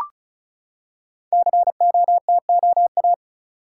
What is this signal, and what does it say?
Morse code, heard as a keyed tone, about 36 wpm, 690 Hz. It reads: COTOA
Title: Code Groups - 5 Letters - 35wpm